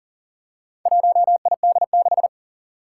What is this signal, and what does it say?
Morse code, heard as a keyed tone, about 40 wpm, 700 Hz. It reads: 1ID6
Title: Four Challenging Letter-Number Combinations 40wpm